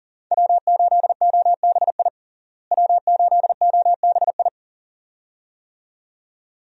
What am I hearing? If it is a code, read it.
Morse code, heard as a keyed tone, about 40 wpm, 700 Hz. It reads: W8OBI W8OBI
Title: USA Call Signs - 5 Characters - Repeated 2x - 40wpm